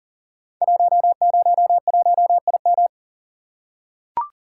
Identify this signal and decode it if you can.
Morse code, heard as a keyed tone, about 40 wpm, 700 Hz. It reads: 101IM
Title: Code Groups - 5 Letter:Numbers - 40wpm